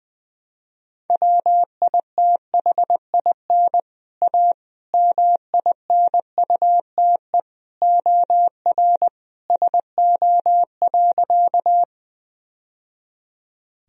Morse code, heard as a keyed tone, about 21 wpm, 700 Hz. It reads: WITHIN A MINUTE OR SO.
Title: Sentences from Top 1000 Words - Review Part 4 - 20wpm